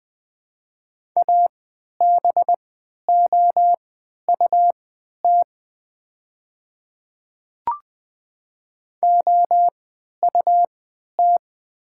Morse code, heard as a keyed tone, about 20 wpm, spaced as for 11 wpm, 700 Hz. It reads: ABOUT OUT
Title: 3x Character Spacing - Top 100 Words - 20wpm